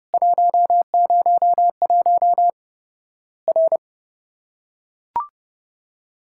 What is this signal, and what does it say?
Morse code, heard as a keyed tone, about 30 wpm, 700 Hz. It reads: 101 R
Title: Worked All Europe DX Contest - QTC Exchange - 30wpm